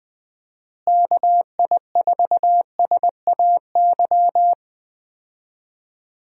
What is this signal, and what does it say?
Morse code, heard as a keyed tone, about 21 wpm, 700 Hz. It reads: KI4SAY
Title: USA Call Signs - 6 Characters - Repeated 2x - 20wpm